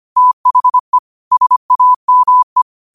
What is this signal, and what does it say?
Morse code, heard as a keyed tone, about 25 wpm, 1000 Hz. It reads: THE SAME